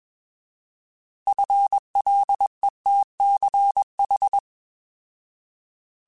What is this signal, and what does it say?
Morse code, heard as a keyed tone, about 21 wpm, 800 Hz. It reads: FLETCH
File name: Alphabetical / Alphabetical mp3s / Fs @20